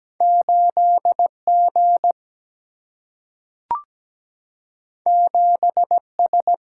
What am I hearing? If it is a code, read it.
Morse code, heard as a keyed tone, about 17 wpm, 700 Hz. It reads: 8G 7S
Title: Single-Letter Single-Number 17wpm